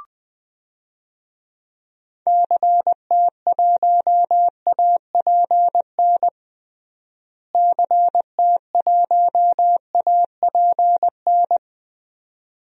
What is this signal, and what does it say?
Morse code, heard as a keyed tone, about 20 wpm, 690 Hz. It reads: CT1APN CT1APN
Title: DXCC-100 Call Signs - 6 Characters - Repeated 2x - 20wpm